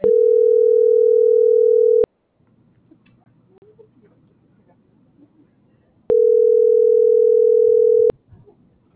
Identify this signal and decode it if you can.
Ringback tone (North America), 440 + 480 Hz.